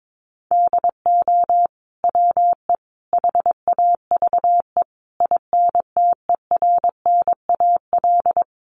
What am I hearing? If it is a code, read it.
Morse code, heard as a keyed tone, about 22 wpm, 700 Hz. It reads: DO WE HAVE INTERNAL
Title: Sentences from Top 1400 Words 22wpm